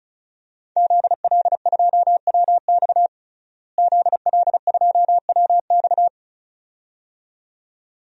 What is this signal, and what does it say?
Morse code, heard as a keyed tone, about 34 wpm, 700 Hz. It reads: ZL2WX ZL2WX
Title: DXCC-100 Call Signs - 5 Characters - Repeated 2x - 35wpm